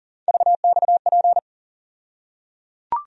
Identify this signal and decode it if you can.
Morse code, heard as a keyed tone, about 37 wpm, 700 Hz. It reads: VXP
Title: USA Call Sign Postfixes - 3 Letters 40wpm